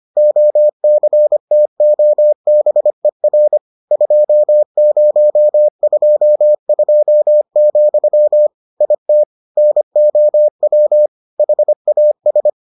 Morse code, heard as a keyed tone, about 25 wpm, 600 Hz. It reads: OCTOBER 2022, IT NOW HAS